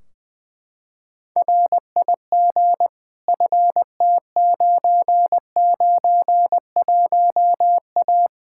Morse code, heard as a keyed tone, about 20 wpm, 700 Hz. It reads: RIG FT991A
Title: K1USN SST QSO Practice 20wpm